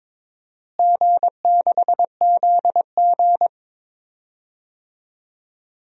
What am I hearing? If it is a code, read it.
Morse code, heard as a keyed tone, about 22 wpm, 700 Hz. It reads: G6ZG
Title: DXCC-100 Call Signs - 4 Characters - Repeated 3x - 22wpm